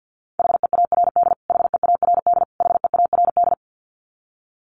Morse code, heard as a keyed tone, about 50 wpm, 700 Hz. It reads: HEARD HEARD HEARD
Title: Top 500 Words Repeated 50wpm